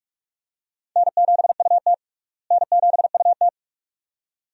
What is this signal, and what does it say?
Morse code, heard as a keyed tone, about 43 wpm, 700 Hz. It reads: N7UT N7UT
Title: USA Call Signs - 4 Characters - Repeated 2x - 45wpm